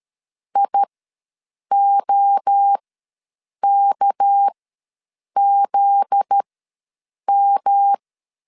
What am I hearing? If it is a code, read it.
Morse code, heard as a keyed tone, about 13 wpm, spaced as for 7 wpm, 800 Hz. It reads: IOKZM
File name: 7 GF9Z ws 12wpm